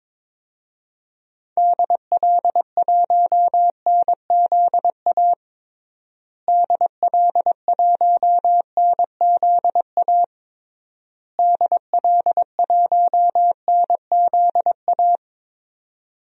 Morse code, heard as a keyed tone, about 22 wpm, 700 Hz. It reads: DL1NZA DL1NZA DL1NZA
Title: DXCC-100 Call Signs - 6 Characters - Repeated 3x - 22wpm